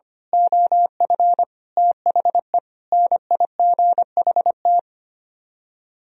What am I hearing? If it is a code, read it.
Morse code, heard as a keyed tone, about 24 wpm, 700 Hz. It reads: OF THE NIGHT
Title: Speed-Racing - Sets of 5 Words 25wpm